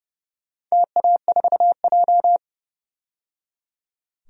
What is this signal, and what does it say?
Morse code, heard as a keyed tone, about 30 wpm, 700 Hz. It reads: TA4J
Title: DXCC-100 Call Signs - 4 Characters - 30wpm